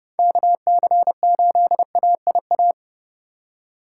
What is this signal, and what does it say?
Morse code, heard as a keyed tone, about 32 wpm, 700 Hz. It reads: KC8AIA
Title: USA Call Signs - 6 Characters - 3x Character Spacing - 30wpm